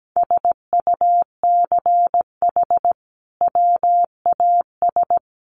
Morse code, heard as a keyed tone, about 17 wpm, 700 Hz. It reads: SUCH WAS
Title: Sentences from Top 1400 Words 17wpm